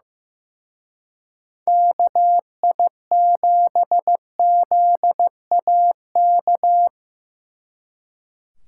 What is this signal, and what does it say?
Morse code, heard as a keyed tone, about 17 wpm, 700 Hz. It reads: KI7ZAK
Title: Speed Racing - USA 2x3 Call Signs 15wpm